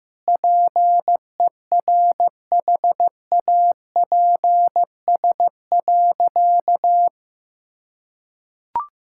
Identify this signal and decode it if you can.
Morse code, heard as a keyed tone, about 15 wpm, 700 Hz. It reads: PERHAPS.
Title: Sentences from Top 1600 Words 15wpm